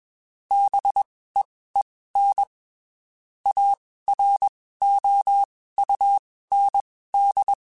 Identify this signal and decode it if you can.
Morse code, heard as a keyed tone, about 21 wpm, spaced as for 15 wpm, 800 Hz. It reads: BEEN AROUND
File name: (spaced) / QSOs Spaced-mp3s / WD0FBT @20 sp